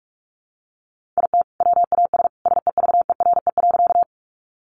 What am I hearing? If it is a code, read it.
Morse code, heard as a keyed tone, about 44 wpm, 700 Hz. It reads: IT WAS SEVERE.
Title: Sentences from Top 1400 Words 45wpm